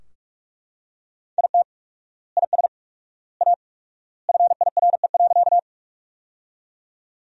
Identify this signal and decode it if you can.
Morse code, heard as a keyed tone, about 43 wpm, 700 Hz. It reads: IT IS A FIRE.
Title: 4x Word-Spacing - Sentences from Top 400 Words - 45wpm